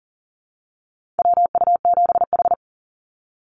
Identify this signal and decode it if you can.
Morse code, heard as a keyed tone, about 40 wpm, 700 Hz. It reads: WU7H
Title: NA SOTA Activator Callsigns 40wpm